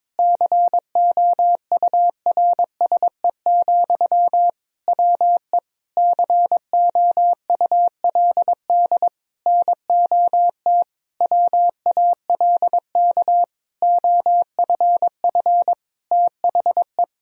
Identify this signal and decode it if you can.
Morse code, heard as a keyed tone, about 22 wpm, 700 Hz. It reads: COURSE, WE COULD NOT WALK OFF THE
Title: Sentences from Top 1000 Words - Review Part 2 - 22wpm